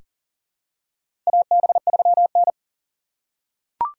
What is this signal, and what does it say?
Morse code, heard as a keyed tone, about 40 wpm, 700 Hz. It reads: AB3N
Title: USA Call Signs - 6x Character Spacing - 40wpm